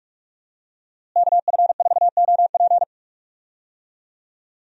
Morse code, heard as a keyed tone, about 46 wpm, 700 Hz. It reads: KF4OP
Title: USA Call Signs - 5 Characters - Repeated 3x - 45wpm